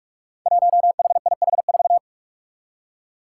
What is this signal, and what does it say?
Morse code, heard as a keyed tone, about 46 wpm, 700 Hz. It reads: 1HIH4
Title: Five Challenging Letter-Number Combinations 45wpm